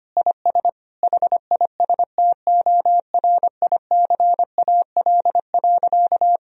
Morse code, heard as a keyed tone, about 25 wpm, 700 Hz. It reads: IS HISTORICAL.
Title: Sentences from Top 1000 Words - Review Part 4 - 25wpm